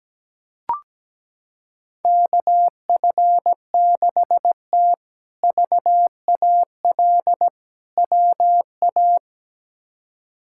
Morse code, heard as a keyed tone, about 17 wpm, 700 Hz. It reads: KF6T VAL WA
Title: K1USN SST QSO Practice - 17wpm